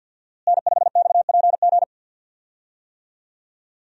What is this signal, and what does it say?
Morse code, heard as a keyed tone, about 52 wpm, 700 Hz. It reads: N5XPG
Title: USA Call Signs - 5 Characters - Repeated 3x - 50wpm